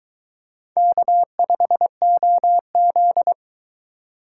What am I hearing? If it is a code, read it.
Morse code, heard as a keyed tone, about 21 wpm, 700 Hz. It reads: K5OZ